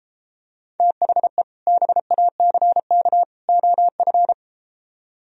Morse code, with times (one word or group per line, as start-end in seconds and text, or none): 0.8-1.4 THE
1.7-3.2 BACK
3.5-4.3 OF